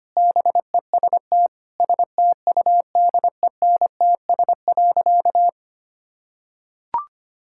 Morse code, heard as a keyed tone, about 25 wpm, 700 Hz. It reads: BEST STUDENTS.
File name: practice / 25wpm / Sentences with 4-Word Phrases - 25wpm